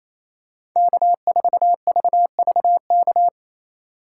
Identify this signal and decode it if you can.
Morse code, heard as a keyed tone, about 28 wpm, 700 Hz. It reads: K4VVK
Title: USA Call Signs - 5 Characters - 4x Character Spacing - 28wpm